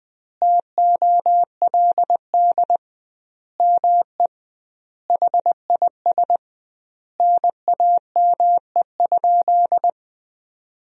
Morse code, heard as a keyed tone, about 20 wpm, 700 Hz. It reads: TOLD ME HIS NAME?
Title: 2x Word-Spacing - Sentences from Top 400 Words - 20wpm